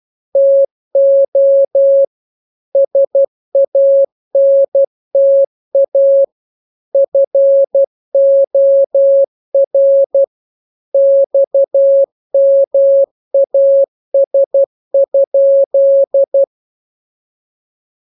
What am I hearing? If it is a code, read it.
Morse code, heard as a keyed tone, about 12 wpm, 550 Hz. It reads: TO SANTA FOR XMAS?
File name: jokes / joke 16 12 0000